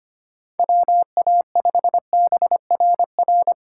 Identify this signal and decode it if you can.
Morse code, heard as a keyed tone, about 25 wpm, 700 Hz. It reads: WA5BRR
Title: USA Call Signs - 6 Characters - 3x Character Spacing - 25wpm